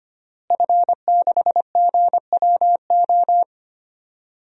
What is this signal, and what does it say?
Morse code, heard as a keyed tone, about 25 wpm, 700 Hz. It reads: F6GWO
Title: DXCC-100 Call Signs - 5 Characters - Repeated 3x - 25wpm